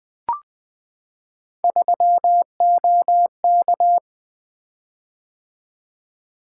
Morse code, heard as a keyed tone, about 20 wpm, 700 Hz. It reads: 3OK
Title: Three Letter-Number Combinations - 20wpm